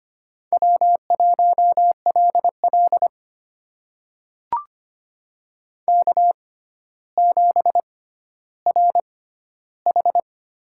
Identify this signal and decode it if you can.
Morse code, heard as a keyed tone, about 25 wpm, spaced as for 13 wpm, 700 Hz. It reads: W1LL K7RH
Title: USA Call Signs - 4 Characters - 6x Character Spacing - 25wpm